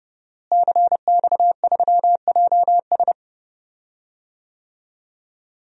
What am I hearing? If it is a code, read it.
Morse code, heard as a keyed tone, about 30 wpm, 700 Hz. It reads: CX3JS
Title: DXCC-100 Call Signs - 5 Characters - Repeated 2x - 30wpm